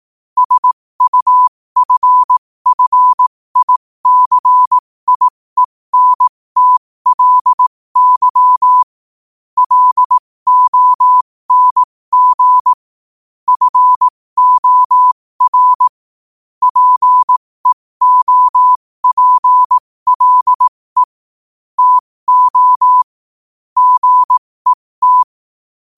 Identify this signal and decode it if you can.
Morse code, heard as a keyed tone, about 18 wpm, spaced as for 15 wpm, 1000 Hz. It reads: SUFFICIENTLY LONG FOR PEOPLE TO GET
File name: qotdfiles / 15wpm